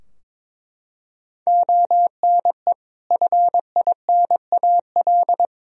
Morse code, heard as a keyed tone, about 22 wpm, 700 Hz. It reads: ONE FINAL
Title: Sentences from Top 1000 Words - Review Part 2 - 22wpm